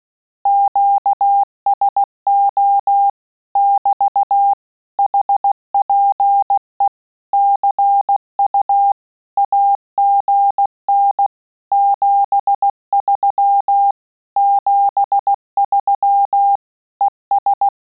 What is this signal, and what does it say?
Morse code, heard as a keyed tone, about 16 wpm, 800 Hz. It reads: QSO <BT> HPE CU AGN 73 73 ES
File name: Telegraphy-group / QSO CW 15 PPM